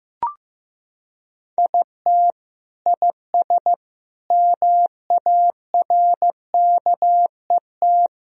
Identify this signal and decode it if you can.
Morse code, heard as a keyed tone, about 15 wpm, 700 Hz. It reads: IT IS MARKET
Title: Sentences from Top 1000 Words - Review Part 3 - 15wpm